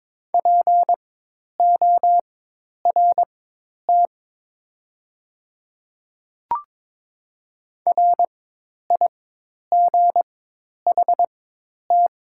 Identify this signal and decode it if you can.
Morse code, heard as a keyed tone, about 22 wpm, spaced as for 10 wpm, 700 Hz. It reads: PORT RIGHT
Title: 4x Character Spacing - Top 300 Words - 22wpm